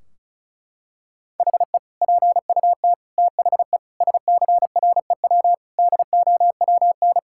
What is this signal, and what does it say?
Morse code, heard as a keyed tone, about 36 wpm, 700 Hz. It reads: HE PUT THE SCREW DOWN
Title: Sentences from Top 1600 Words 35wpm